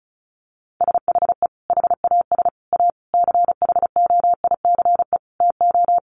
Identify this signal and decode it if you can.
Morse code, heard as a keyed tone, about 34 wpm, 700 Hz. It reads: SHE HAS A CHOICE TO
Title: Sentences from Top 900 Words 35wpm